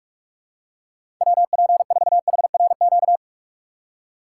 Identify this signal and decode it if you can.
Morse code, heard as a keyed tone, about 42 wpm, 700 Hz. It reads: WP4HRQ
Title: USA Call Signs - 6 Characters - 4x Character Spacing - 45wpm